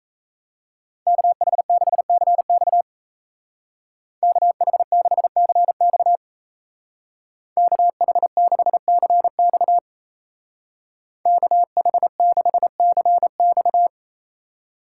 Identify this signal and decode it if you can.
Morse code, heard as a keyed tone, about 35 wpm, 700 Hz. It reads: KH6CX KH6CX KH6CX KH6CX
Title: Speed Racing - USA 2x2 Call Signs 28wpm